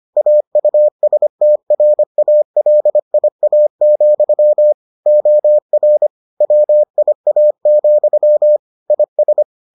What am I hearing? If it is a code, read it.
Morse code, heard as a keyed tone, about 25 wpm, 600 Hz. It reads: AUSTRALIA, OR WIA, IS